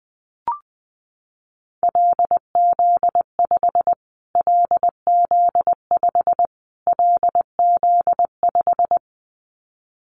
Morse code, heard as a keyed tone, about 20 wpm, 700 Hz. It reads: LZ5 LZ5 LZ5
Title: Callsign Prefixes for Top 25 DXCC - Repeated 3x - 20wpm